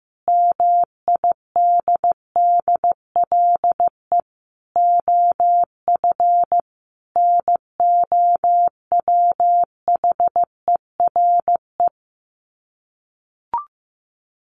Morse code, heard as a keyed tone, about 15 wpm, 700 Hz. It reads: MIDDLE OF NOWHERE
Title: Sets of 5 Words Encore 15wpm